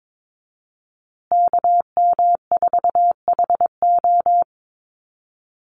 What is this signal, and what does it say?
Morse code, heard as a keyed tone, about 22 wpm, 700 Hz. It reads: KM4HO
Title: USA Call Signs Repeated 3x 22wpm